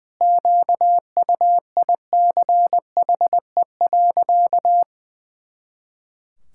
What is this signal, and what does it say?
Morse code, heard as a keyed tone, about 20 wpm, 700 Hz. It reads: QUICHE.
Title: Pangram Challenge v2 - 20wpm